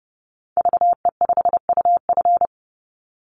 Morse code, heard as a keyed tone, about 30 wpm, 700 Hz. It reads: VE5UF
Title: Call Signs Repeated 2x - 30wpm